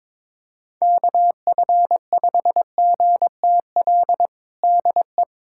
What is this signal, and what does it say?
Morse code, heard as a keyed tone, about 22 wpm, 700 Hz. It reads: KF5GTL DE
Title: USA Callsign DE Callsign 22wpm